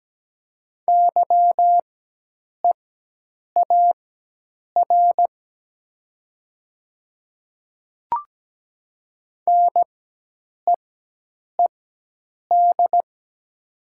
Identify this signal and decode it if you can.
Morse code, heard as a keyed tone, about 17 wpm, spaced as for 8 wpm, 700 Hz. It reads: YEAR NEED
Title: 4x Character Spacing - Top 300 Words - 17wpm